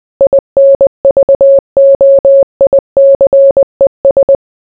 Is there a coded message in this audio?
Yes: Morse code from 0.2 to 4.4 s, heard as a keyed tone, about 20 wpm, 550 Hz. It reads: INVOICES